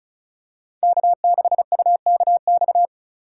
Morse code, heard as a keyed tone, about 36 wpm, 700 Hz. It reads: K6UKX
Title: USA Call Signs - 2x Character Spacing - 35wpm